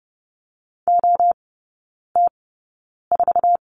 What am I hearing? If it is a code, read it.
Morse code, heard as a keyed tone, about 30 wpm, spaced as for 9 wpm, 700 Hz. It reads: OT4
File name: course / 30:10wpm / 087 Lesson 15e - Number 5 - Three Characters 30wpm